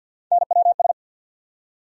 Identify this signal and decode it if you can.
Morse code, heard as a keyed tone, about 49 wpm, 700 Hz. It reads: NWS
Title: USA Call Sign Postfixes - 3 Letters 50wpm